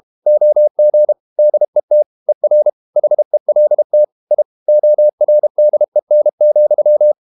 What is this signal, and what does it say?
Morse code, heard as a keyed tone, about 32 wpm, 600 Hz. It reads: OG DET ER HELT I ORDEN,
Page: Ferie 32 wpm